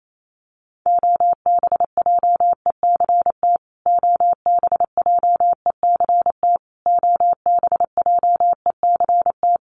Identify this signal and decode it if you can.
Morse code, heard as a keyed tone, about 28 wpm, 700 Hz. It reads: OBJECT OBJECT OBJECT
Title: Top 500 Words Repeated 28wpm